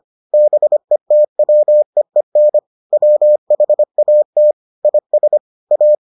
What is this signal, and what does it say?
Morse code, heard as a keyed tone, about 25 wpm, 600 Hz. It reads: BETWEEN WHAT IS A